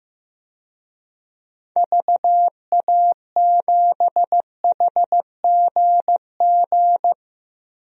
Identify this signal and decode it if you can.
Morse code, heard as a keyed tone, about 15 wpm, 700 Hz. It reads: VA7HGG
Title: DXCC-100 Call Signs - 6 Characters - 15wpm